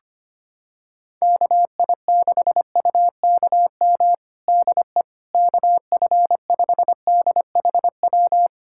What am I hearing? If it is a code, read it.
Morse code, heard as a keyed tone, about 25 wpm, 700 Hz. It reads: KI6UKM DE KF5DHW
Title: USA Callsign DE Callsign - 6 Characters - 25wpm